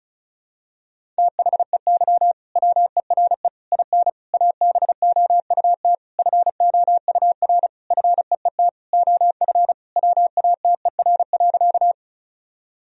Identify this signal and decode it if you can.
Morse code, heard as a keyed tone, about 34 wpm, 700 Hz. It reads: THEY WERE IN ABOUT FOUR FEET OF WATER.
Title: Sentences from Top 1000 Words - Review Part 1 - 35wpm